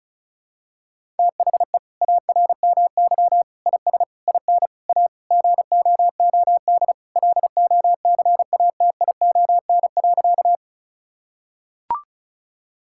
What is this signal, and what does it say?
Morse code, heard as a keyed tone, about 36 wpm, 700 Hz. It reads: THE ARMY IS IN A GOOD LOCATION.
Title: Sentences from Top 1000 Words - Review Part 6 - 35wpm